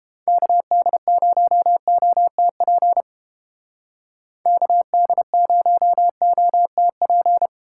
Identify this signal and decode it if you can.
Morse code, heard as a keyed tone, about 29 wpm, 700 Hz. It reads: KD0OTP KD0OTP